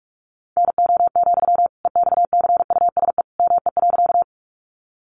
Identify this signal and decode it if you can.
Morse code, heard as a keyed tone, about 44 wpm, 700 Hz. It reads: NO, EXCUSE ME.
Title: Sentences from Top 700 Words 45wpm